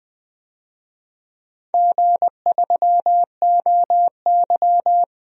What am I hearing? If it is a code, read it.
Morse code, heard as a keyed tone, about 20 wpm, 700 Hz. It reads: G3OY
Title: DXCC-100 Call Signs - 4 Characters - 20wpm